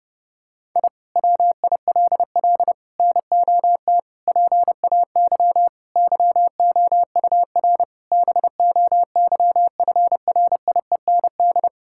Morse code, heard as a keyed tone, about 30 wpm, 700 Hz. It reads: I WILL NOT PAY YOUR BOYFRIEND
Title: Sentences from Top 1000 Words - Review Part 5 - 30wpm